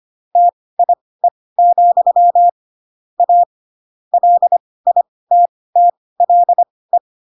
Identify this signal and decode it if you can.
Morse code, heard as a keyed tone, about 25 wpm, spaced as for 18 wpm, 700 Hz. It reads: TIE, A LITTLE